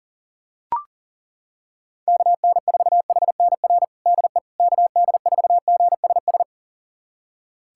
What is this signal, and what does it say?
Morse code, heard as a keyed tone, about 40 wpm, 700 Hz. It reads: KN4HNR DE KD4GSS
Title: USA Callsign DE Callsign - 6 Characters - 40wpm